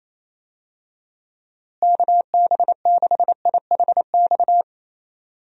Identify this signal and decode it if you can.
Morse code, heard as a keyed tone, about 28 wpm, 700 Hz. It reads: KB6IHX
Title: USA Call Signs - 6 Characters - Repeated 3x - 28wpm